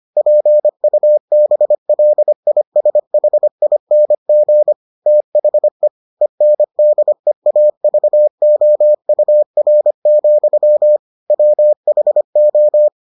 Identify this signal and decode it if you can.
Morse code, heard as a keyed tone, about 25 wpm, 600 Hz. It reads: PUBLISHING THE ENDEAVOUR, WHO